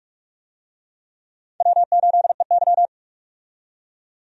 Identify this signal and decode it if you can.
Morse code, heard as a keyed tone, about 41 wpm, 700 Hz. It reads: W8EY
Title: USA Call Signs - 4 Characters - 45wpm